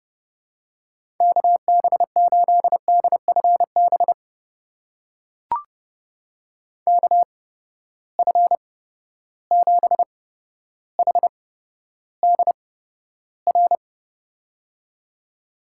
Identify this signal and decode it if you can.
Morse code, heard as a keyed tone, about 31 wpm, spaced as for 13 wpm, 700 Hz. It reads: KB8DFB KF7HDR
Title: USA Call Signs - 6 Characters - 8x Character Spacing - 30wpm